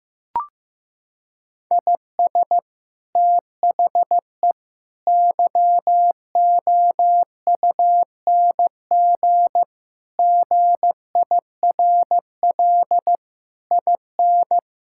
Morse code, heard as a keyed tone, about 15 wpm, 700 Hz. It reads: IS THE YOUNG GIRL IN
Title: Sentences from Top 1000 Words - Review Part 5 - 15wpm